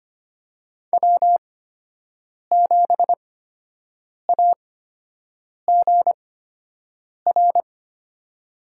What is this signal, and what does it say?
Morse code, heard as a keyed tone, about 25 wpm, spaced as for 7 wpm, 700 Hz. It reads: W7AGR
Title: USA Call Signs - 5 Characters - 8x Character Spacing - 25wpm